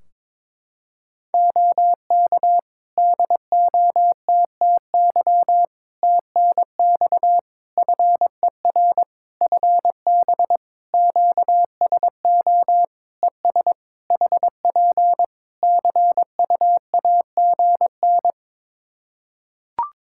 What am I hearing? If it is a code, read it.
Morse code, heard as a keyed tone, about 22 wpm, 700 Hz. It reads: OK DOTTY TNX FER FB QSO ES HP CUAGN
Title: Practice QSO - Exchange 4 - Ending - 22wpm